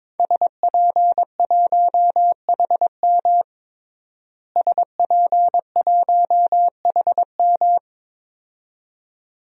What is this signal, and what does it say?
Morse code, heard as a keyed tone, about 22 wpm, 700 Hz. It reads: SP1HM SP1HM
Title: DXCC-100 Call Signs - 5 Characters - Repeated 2x - 22wpm